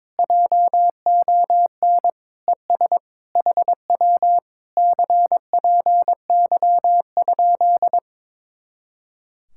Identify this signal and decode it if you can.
Morse code, heard as a keyed tone, about 22 wpm, 700 Hz. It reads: JON ES HW CPY?
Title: Speed-Racing Common QSO Phrases - 22wpm